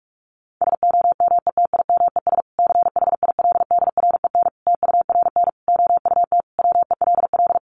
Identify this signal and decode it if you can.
Morse code, heard as a keyed tone, about 44 wpm, 700 Hz. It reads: SOMETIMES CHILDREN TURN OUT WELL
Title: Sentences from Top 600 Words 45wpm